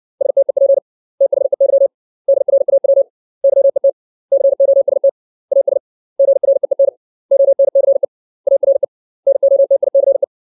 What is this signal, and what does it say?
Morse code, heard as a keyed tone, about 60 wpm, 550 Hz. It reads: STEP THEY DRAW YET COST AS GREEN ONCE ARE NOTICE
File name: 632 words / 632 Words 60WPM